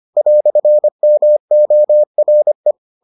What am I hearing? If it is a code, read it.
Morse code, heard as a keyed tone, about 25 wpm, 600 Hz. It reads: "MORE